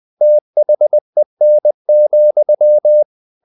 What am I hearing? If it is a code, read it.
Morse code, heard as a keyed tone, about 20 wpm, 600 Hz. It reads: THEN,
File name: the old man and the sea / the old man and the sea part 1 20wpm